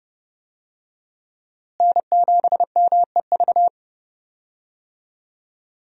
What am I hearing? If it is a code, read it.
Morse code, heard as a keyed tone, about 30 wpm, 700 Hz. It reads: N7MEV
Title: USA Call Signs - 5 Characters - Repeated 2x - 30wpm